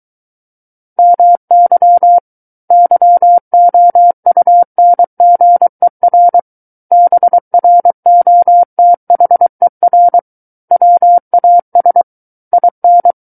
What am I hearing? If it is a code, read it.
Morse code, heard as a keyed tone, about 23 wpm, 700 Hz. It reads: MY YOUNGER BROTHER WAS IN
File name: Book One / 14-Chapter 14 - 23 WPM